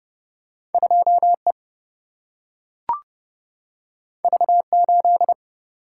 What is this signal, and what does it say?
Morse code, heard as a keyed tone, about 32 wpm, 700 Hz. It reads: 2 E V 8
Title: Single-Letter Single-Number 30wpm